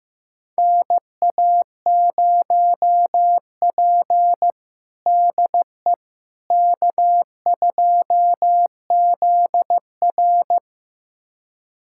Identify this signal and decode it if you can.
Morse code, heard as a keyed tone, about 15 wpm, 700 Hz. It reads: NA0P DE K2ZR
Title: USA Callsign DE Callsign - 4 Characters - 15wpm